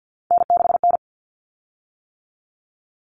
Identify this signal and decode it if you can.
Morse code, heard as a keyed tone, about 50 wpm, 700 Hz. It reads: N6N
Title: Common Cut Numbers - 3-Digit 50wpm